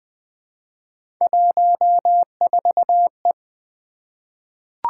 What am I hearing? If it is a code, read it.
Morse code, heard as a keyed tone, about 20 wpm, 700 Hz. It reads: 14E
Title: Three Letter-Number Combinations - 20wpm